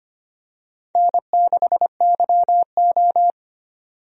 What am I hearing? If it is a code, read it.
Morse code, heard as a keyed tone, about 25 wpm, 700 Hz. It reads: N6YO
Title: USA Call Signs - 4 Characters - 25wpm